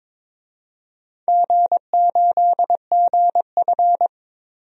Morse code, heard as a keyed tone, about 22 wpm, 700 Hz. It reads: G8GF
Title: DXCC-100 Call Signs - 4 Characters - Repeated 2x - 22wpm